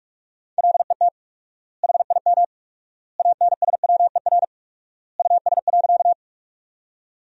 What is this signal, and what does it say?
Morse code, heard as a keyed tone, about 46 wpm, 700 Hz. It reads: LET HIM ANSWER US.
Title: 4x Word-Spacing - Sentences from Top 300 Words - 45wpm